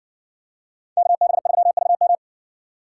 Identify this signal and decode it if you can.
Morse code, heard as a keyed tone, about 60 wpm, 700 Hz. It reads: KB3VM